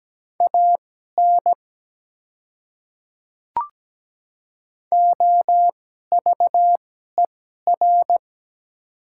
Morse code, heard as a keyed tone, about 17 wpm, spaced as for 12 wpm, 700 Hz. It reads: AN OVER
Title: 2x Character Spacing - Top 100 Words - 17wpm